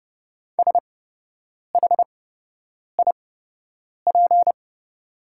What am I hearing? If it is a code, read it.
Morse code, heard as a keyed tone, about 30 wpm, spaced as for 8 wpm, 700 Hz. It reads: SHIP
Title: 8x Character Spacing - Top 500 Words - 30wpm